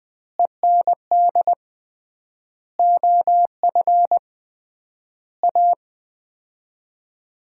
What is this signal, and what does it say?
Morse code, heard as a keyed tone, about 21 wpm, 690 Hz. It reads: END OF A
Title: 3x Word-Spacing - Sets of 5 Words - 20wpm